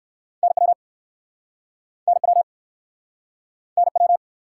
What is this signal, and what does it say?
Morse code, heard as a keyed tone, about 61 wpm, 700 Hz. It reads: NW NW NW